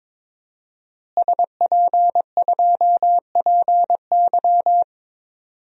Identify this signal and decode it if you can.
Morse code, heard as a keyed tone, about 22 wpm, 700 Hz. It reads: SP2PY
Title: DXCC-100 Call Signs - 5 Characters - Repeated 2x - 22wpm